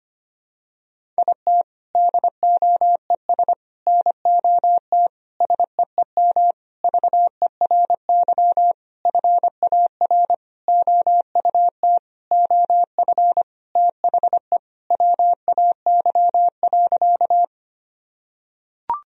Morse code, heard as a keyed tone, about 25 wpm, 700 Hz. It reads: IT DOES NOT SEEM VERY FAR OUT OF THE WAY.
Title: Sentences from Top 1000 Words - Review Part 1 - 25wpm